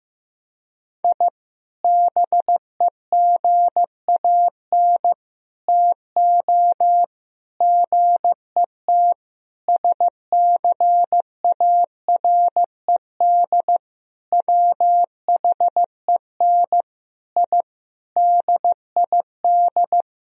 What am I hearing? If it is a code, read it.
Morse code, heard as a keyed tone, about 15 wpm, 700 Hz. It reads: I BEGAN TO GET SCARED WHEN I DID
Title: Sentences from Top 1000 Words - Review Part 4 - 15wpm